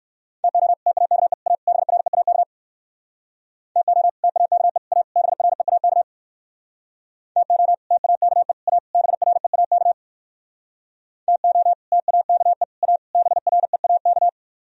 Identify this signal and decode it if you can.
Morse code, heard as a keyed tone, about 57 wpm, 700 Hz. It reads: TO TAKE A BREAK TO TAKE A BREAK TO TAKE A BREAK TO TAKE A BREAK